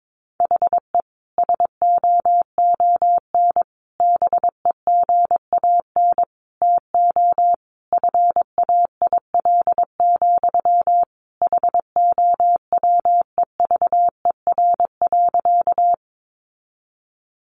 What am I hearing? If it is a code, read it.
Morse code, heard as a keyed tone, about 22 wpm, 700 Hz. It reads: HE SOON BEGAN TO FAIL, HOWEVER.
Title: Sentences from Top 1100 Words 22wpm